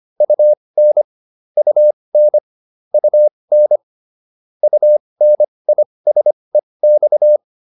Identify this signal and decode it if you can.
Morse code, heard as a keyed tone, about 25 wpm, spaced as for 20 wpm, 600 Hz. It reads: UN UN UN UNISEX
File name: UN6 20